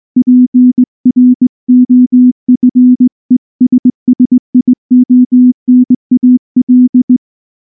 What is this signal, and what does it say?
Morse code, heard as a keyed tone, about 18 wpm, 250 Hz. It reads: PROFESSIONAL
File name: morse-training-20WPM